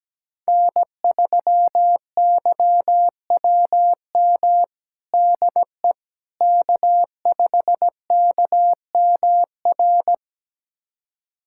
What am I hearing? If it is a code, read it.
Morse code, heard as a keyed tone, about 17 wpm, 700 Hz. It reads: N3YWM DE K5KMR
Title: USA Callsign DE Callsign - 5 Characters - 17wpm